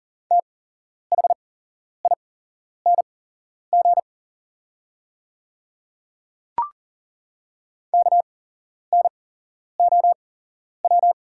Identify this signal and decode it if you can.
Morse code, heard as a keyed tone, about 40 wpm, spaced as for 11 wpm, 690 Hz. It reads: THING KNOW
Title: 8x Character Spacing - Top 200 Words - 40wpm